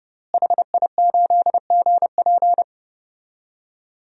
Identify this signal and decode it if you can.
Morse code, heard as a keyed tone, about 30 wpm, 700 Hz. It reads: HI8GP
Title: DXCC-100 Call Signs - 5 Characters - Repeated 2x - 30wpm